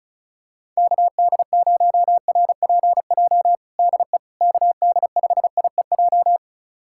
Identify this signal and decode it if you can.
Morse code, heard as a keyed tone, about 36 wpm, 700 Hz. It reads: KD0RPJ DE KD5IEJ
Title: USA Callsign DE Callsign - 6 Characters - 35wpm